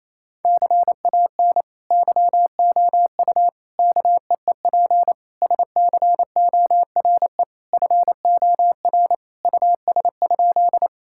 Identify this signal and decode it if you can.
Morse code, heard as a keyed tone, about 28 wpm, 700 Hz. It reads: CAN YOU KEEP SCORE FOR US?
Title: Sentences from Top 1600 Words 28wpm